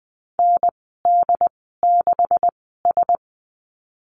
Morse code, heard as a keyed tone, about 20 wpm, spaced as for 15 wpm, 700 Hz. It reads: ND6S
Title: 147 Lesson 24g - Letter y - Call Signs 20wpm